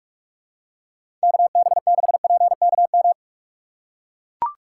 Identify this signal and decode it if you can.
Morse code, heard as a keyed tone, about 44 wpm, 700 Hz. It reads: KB6PKM
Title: USA Call Signs - 3x Character Spacing - 45wpm